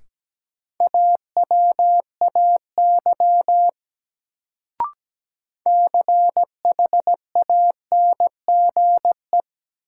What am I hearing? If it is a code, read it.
Morse code, heard as a keyed tone, about 17 wpm, 700 Hz. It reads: AWAY CHANGE
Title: Sentences from Top 600 Words - New Word Drill - 17wpm